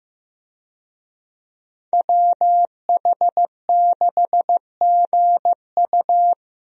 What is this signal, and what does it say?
Morse code, heard as a keyed tone, about 15 wpm, 700 Hz. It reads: WH6GU
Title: USA Call Signs - 5 Characters - 15wpm